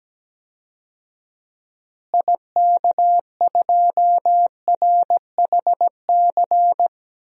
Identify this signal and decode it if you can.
Morse code, heard as a keyed tone, about 17 wpm, 700 Hz. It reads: IK2RHC
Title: DXCC-100 Call Signs - 6 Characters - Repeated 2x - 17wpm